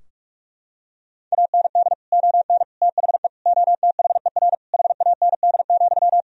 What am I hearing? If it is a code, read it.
Morse code, heard as a keyed tone, about 42 wpm, 700 Hz. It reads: AND ON THE OTHER HAND,
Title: Sentences from Top 1000 Words - Review Part 3 - 45wpm